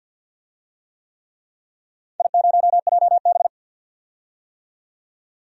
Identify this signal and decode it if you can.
Morse code, heard as a keyed tone, about 52 wpm, 700 Hz. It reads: I0JB
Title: DXCC-100 Call Signs - 4 Characters - 50wpm